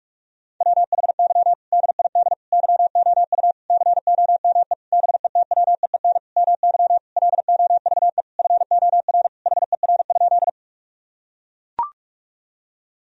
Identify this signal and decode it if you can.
Morse code, heard as a keyed tone, about 41 wpm, 700 Hz. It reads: WHY DID YOU COME BETWEEN MY LOVE FOR HER?
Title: Sentences from Top 1000 Words - Review Part 1 - 45wpm